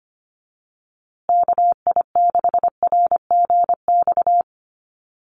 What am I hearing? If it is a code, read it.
Morse code, heard as a keyed tone, about 25 wpm, 700 Hz. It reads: KI6RGX
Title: USA Call Signs Repeated 2x 25wpm